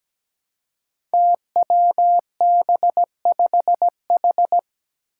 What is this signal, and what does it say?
Morse code, heard as a keyed tone, about 17 wpm, 700 Hz. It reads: TWB5H
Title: Five Challenging Letter-Number Combinations 17wpm